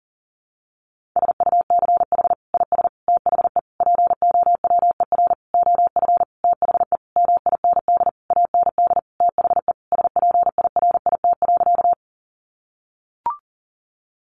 Morse code, heard as a keyed tone, about 40 wpm, 700 Hz. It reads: SUCH IS THE POWER OF THE MIND AND THE SPIRIT.
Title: Sentences from Top 1400 Words 40wpm